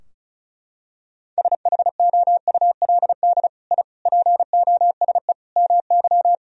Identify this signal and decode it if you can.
Morse code, heard as a keyed tone, about 36 wpm, 700 Hz. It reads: SHOULD I POSE MY
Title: Sentences from Top 1000 Words - Review Part 2 - 35wpm